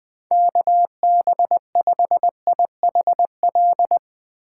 Morse code, heard as a keyed tone, about 20 wpm, 700 Hz. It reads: KB5IHL
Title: Speed Racing - USA 2x3 Call Signs 17wpm